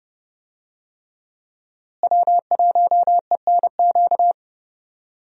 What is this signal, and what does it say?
Morse code, heard as a keyed tone, about 30 wpm, 700 Hz. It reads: W1ENQ
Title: USA Call Signs - 5 Characters - Repeated 3x - 30wpm